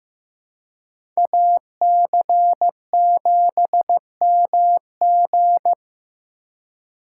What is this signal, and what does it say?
Morse code, heard as a keyed tone, about 15 wpm, 700 Hz. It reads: AC7MG
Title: USA Call Signs - 2x Character Spacing - 15wpm